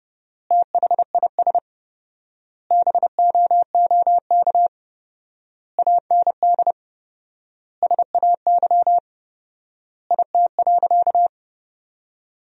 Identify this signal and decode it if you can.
Morse code, heard as a keyed tone, about 30 wpm, 700 Hz. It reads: THIS BOOK AND SAY IT.
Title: 4x Word-Spacing - Sentences from Top 400 Words - 30wpm